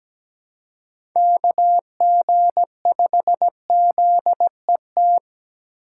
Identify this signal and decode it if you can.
Morse code, heard as a keyed tone, about 20 wpm, 700 Hz. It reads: KG5ZET
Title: Speed Racing - USA 2x3 Call Signs 17wpm